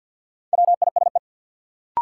Morse code, heard as a keyed tone, about 52 wpm, 700 Hz. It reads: WISE
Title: Sentences from Top 1600 Words - New Word Drill Alt - 50wpm